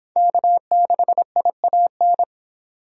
Morse code, heard as a keyed tone, about 24 wpm, 700 Hz. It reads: K6IAN